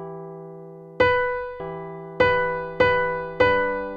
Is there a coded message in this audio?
No.